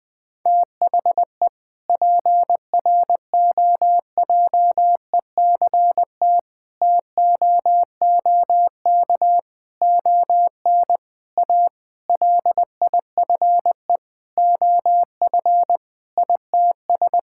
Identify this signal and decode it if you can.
Morse code, heard as a keyed tone, about 20 wpm, 700 Hz. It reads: THE PROJECT TOOK ON A LIFE OF ITS
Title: Sentences with 7-Word Phrases - Spoken Beginning - 20wpm